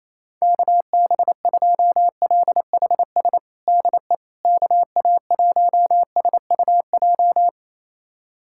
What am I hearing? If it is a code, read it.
Morse code, heard as a keyed tone, about 28 wpm, 700 Hz. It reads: KB2LHS DE KA1SUJ
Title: USA Callsign DE Callsign 28wpm